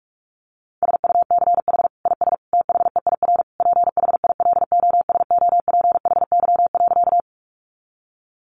Morse code, heard as a keyed tone, about 44 wpm, 700 Hz. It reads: SUCH IS THEIR PHILOSOPHY.
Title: Sentences from Top 900 Words 45wpm